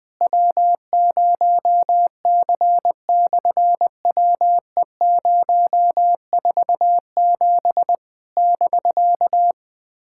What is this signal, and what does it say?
Morse code, heard as a keyed tone, about 20 wpm, 700 Hz. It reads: W0C/WE047 <BK>
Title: SOTA QSO Practice 20wpm